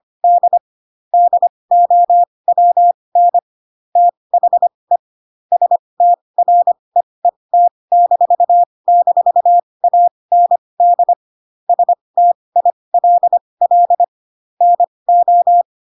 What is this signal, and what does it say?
Morse code, heard as a keyed tone, about 25 wpm, spaced as for 20 wpm, 700 Hz. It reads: D DOWN THE STREET--AND STILL NO
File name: Pinocchio / Pinocchio ch1 15 25 ik1hge